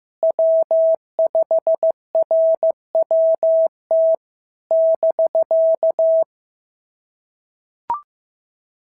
Morse code, heard as a keyed tone, about 15 wpm, 650 Hz. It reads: W5RWT <BK>
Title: SOTA QSO Practice 15wpm